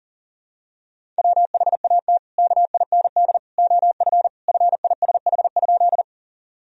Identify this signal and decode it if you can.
Morse code, heard as a keyed tone, about 40 wpm, 700 Hz. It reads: WHAT KIND OF FISH?
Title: Sentences from Top 1000 Words - Review Part 1 - 40wpm